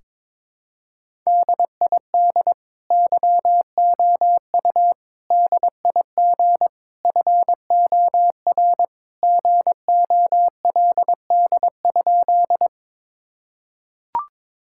Morse code, heard as a keyed tone, about 22 wpm, 700 Hz. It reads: DID YOU DIG FOR GOLD?
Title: Sentences from Top 1600 Words 22wpm